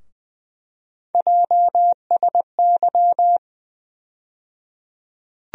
Morse code, heard as a keyed tone, about 20 wpm, 700 Hz. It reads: JSY
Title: USA Call Sign Postfixes - 3 Letters 20wpm